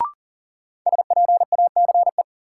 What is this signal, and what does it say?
Morse code, heard as a keyed tone, about 40 wpm, 700 Hz. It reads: SPACE
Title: Sentences from Top 1000 Words - Review Part 5 - New Word Drill - 40wpm